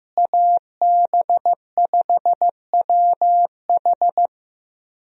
Morse code, heard as a keyed tone, about 15 wpm, 700 Hz. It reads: AB5WH
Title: USA Call Signs - 5 Characters - Repeated 2x - 15wpm